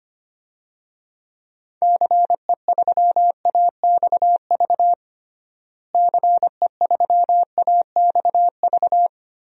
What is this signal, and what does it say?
Morse code, heard as a keyed tone, about 25 wpm, 700 Hz. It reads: CE3AXV CE3AXV
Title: DXCC-100 Call Signs - 6 Characters - Repeated 2x - 25wpm